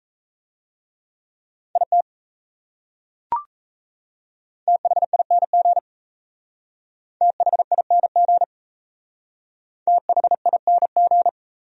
Morse code, heard as a keyed tone, about 36 wpm, 700 Hz. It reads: IT THING THING THING